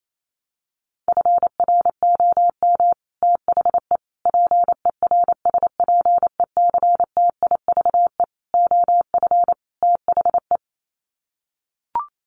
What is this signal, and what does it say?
Morse code, heard as a keyed tone, about 28 wpm, 700 Hz. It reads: FROM THE PERSPECTIVE OF THE
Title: Sets of 5 Words Encore2 28wpm